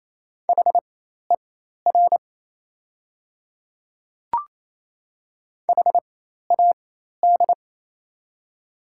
Morse code, heard as a keyed tone, about 28 wpm, spaced as for 13 wpm, 700 Hz. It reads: HER HAD
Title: 4x Character Spacing - Top 100 Words - 28wpm